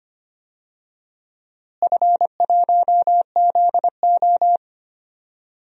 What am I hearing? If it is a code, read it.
Morse code, heard as a keyed tone, about 25 wpm, 700 Hz. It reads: F1ZO
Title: DXCC-100 Call Signs - 4 Characters - Repeated 2x - 25wpm